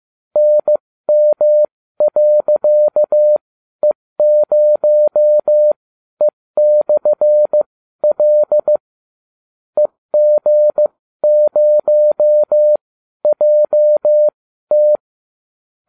Morse code, heard as a keyed tone, about 15 wpm, spaced as for 12 wpm, 600 Hz. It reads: NM.E0E/L EG0JT